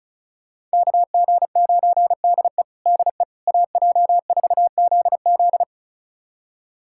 Morse code, heard as a keyed tone, about 34 wpm, 700 Hz. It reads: KG9DE DE AJ4ZZ
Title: USA Callsign DE Callsign - 5 Characters - 35wpm